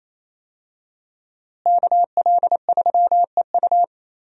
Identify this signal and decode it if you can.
Morse code, heard as a keyed tone, about 28 wpm, 700 Hz. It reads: KL3EU
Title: USA Call Signs - 5 Characters - Repeated 3x - 28wpm